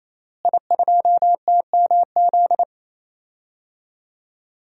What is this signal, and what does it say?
Morse code, heard as a keyed tone, about 28 wpm, 690 Hz. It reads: I2TMZ
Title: DXCC-100 Call Signs - 5 Characters - Repeated 3x - 28wpm